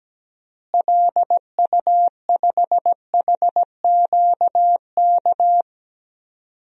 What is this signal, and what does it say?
Morse code, heard as a keyed tone, about 17 wpm, 700 Hz. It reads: LU5HQK
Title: DXCC-100 Call Signs - 6 Characters - Repeated 2x - 17wpm